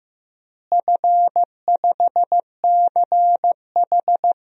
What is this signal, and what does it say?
Morse code, heard as a keyed tone, about 15 wpm, 700 Hz. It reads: F5CH
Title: DXCC-100 Call Signs - 4 Characters - Repeated 3x - 15wpm